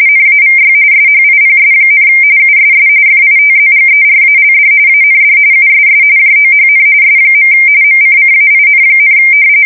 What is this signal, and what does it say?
RTTY, about 45.45 baud. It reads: NAVY SHIPS, USS FITZGERALD AND USS JOHN S. MCCAIN,
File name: fsk170 / usnavy5